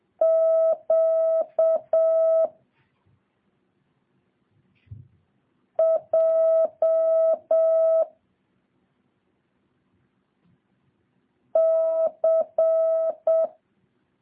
Morse code, heard as a keyed tone, about 7 wpm, 650 Hz. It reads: Q J C
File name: jh8cbh / 2023cw